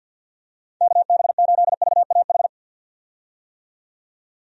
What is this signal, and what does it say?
Morse code, heard as a keyed tone, about 53 wpm, 700 Hz. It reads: KB8VAH
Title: USA Call Signs - 6 Characters - 50wpm